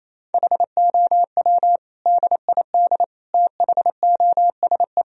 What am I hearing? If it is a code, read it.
Morse code, heard as a keyed tone, about 28 wpm, 700 Hz. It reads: HOW DID THOSE
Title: Sentences from Top 1000 Words - Review Part 1 - 28wpm